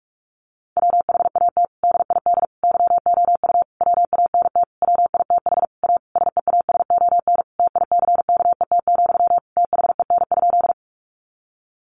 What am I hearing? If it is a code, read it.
Morse code, heard as a keyed tone, about 44 wpm, 700 Hz. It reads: WHAT DID YOU WANT WITH A SEASON TICKET, THEN?
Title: Sentences from Top 1400 Words 45wpm